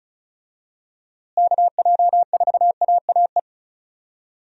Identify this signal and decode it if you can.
Morse code, heard as a keyed tone, about 36 wpm, 700 Hz. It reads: KJ4AAE
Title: USA Call Signs - 6 Characters - Repeated 2x - 35wpm